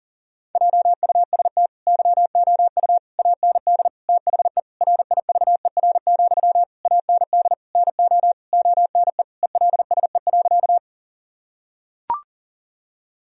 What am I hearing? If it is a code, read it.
Morse code, heard as a keyed tone, about 40 wpm, 700 Hz. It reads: JUST YOU AND THE RIVER, AND NO ONE ELSE.
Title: Sentences from Top 1000 Words - Review Part 1 - 40wpm